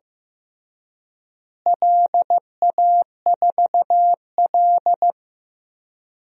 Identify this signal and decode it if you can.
Morse code, heard as a keyed tone, about 15 wpm, 700 Hz. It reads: LA4L
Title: DXCC-100 Call Signs - 4 Characters - Repeated 3x - 15wpm